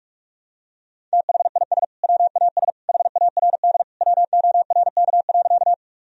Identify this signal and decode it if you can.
Morse code, heard as a keyed tone, about 42 wpm, 700 Hz. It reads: THIS WAS HARD WORK.
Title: Sentences from Top 1000 Words - Review Part 1 - 45wpm